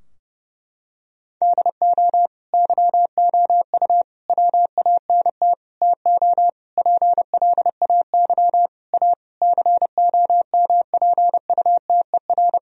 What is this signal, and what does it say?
Morse code, heard as a keyed tone, about 30 wpm, 700 Hz. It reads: DO YOU WANT TO PLAY A COMPUTER
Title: Sentences from Top 1000 Words - Review Part 3 - 30wpm